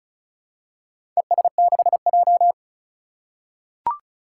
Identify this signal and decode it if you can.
Morse code, heard as a keyed tone, about 36 wpm, 700 Hz. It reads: ES6J
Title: Four Challenging Letter-Number Combinations 35wpm